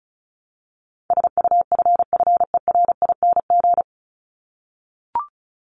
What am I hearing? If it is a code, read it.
Morse code, heard as a keyed tone, about 34 wpm, 700 Hz. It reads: SUFFERING
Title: ING Suffix - Top 100 Words with Base Words 35wpm